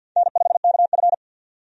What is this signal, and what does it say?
Morse code, heard as a keyed tone, about 52 wpm, 700 Hz. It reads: N5KF
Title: USA Call Signs - 4 Characters - 50wpm